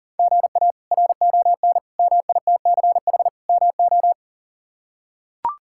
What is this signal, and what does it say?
Morse code, heard as a keyed tone, about 37 wpm, 700 Hz. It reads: GA RON MITCH MO
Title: K1USN SST QSO Practice - 40wpm